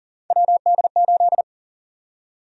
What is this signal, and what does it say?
Morse code, heard as a keyed tone, about 40 wpm, 700 Hz. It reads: WD8
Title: USA Call Sign Prefixes - Repeated 2x - 40wpm